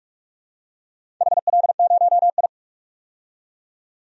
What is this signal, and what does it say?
Morse code, heard as a keyed tone, about 46 wpm, 700 Hz. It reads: HL0I
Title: DXCC-100 Call Signs - 4 Characters - 45wpm